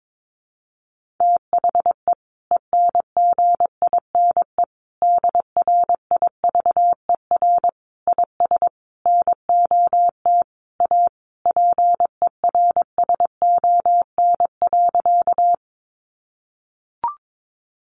Morse code, heard as a keyed tone, about 22 wpm, 700 Hz. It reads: THE ENGINE DRIVER IS NOT A PERSON.
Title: Sentences from Top 1500 Words 22wpm